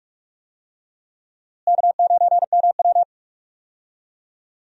Morse code, heard as a keyed tone, about 46 wpm, 700 Hz. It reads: K9MW
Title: USA Call Signs - 4 Characters - Repeated 2x - 45wpm